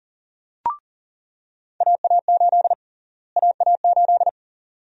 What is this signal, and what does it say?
Morse code, heard as a keyed tone, about 40 wpm, 700 Hz. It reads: AA8 AA8
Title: USA Call Sign Prefixes - Repeated 2x - 40wpm